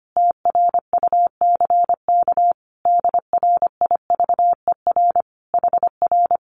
Morse code, heard as a keyed tone, about 25 wpm, 700 Hz. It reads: TRUCK DRIVER HR
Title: Common QSO Phrases Encore 25wpm